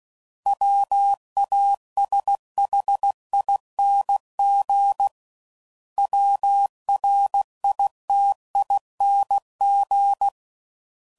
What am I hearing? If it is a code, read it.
Morse code, heard as a keyed tone, about 16 wpm, 800 Hz. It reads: WASHING WRITING
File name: Suffixes / Special INGs / Sngl Ings